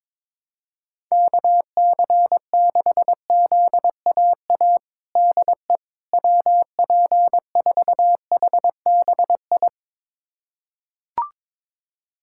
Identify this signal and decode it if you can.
Morse code, heard as a keyed tone, about 22 wpm, 700 Hz. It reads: KC6ZAA DE WP4HBI
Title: USA Callsign DE Callsign - 6 Characters - 22wpm